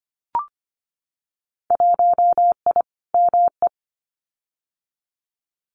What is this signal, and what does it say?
Morse code, heard as a keyed tone, about 25 wpm, 700 Hz. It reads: 1I ME
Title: Winter Field Day Exchanges 25wpm